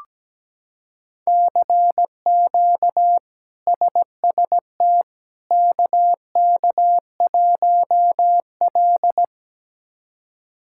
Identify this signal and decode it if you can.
Morse code, heard as a keyed tone, about 17 wpm, 700 Hz. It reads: CQ SST KK1L
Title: K1USN SST QSO Practice - 17wpm